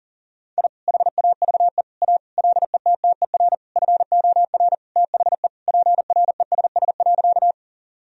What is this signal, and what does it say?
Morse code, heard as a keyed tone, about 40 wpm, 700 Hz. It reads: I HAVE A LETTER FOR THE PRESS.
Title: Sentences from Top 1000 Words - Review Part 1 - 40wpm